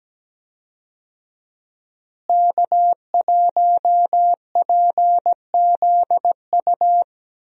Morse code, heard as a keyed tone, about 17 wpm, 700 Hz. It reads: K1PZU
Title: USA Call Signs - 5 Characters - Repeated 2x - 17wpm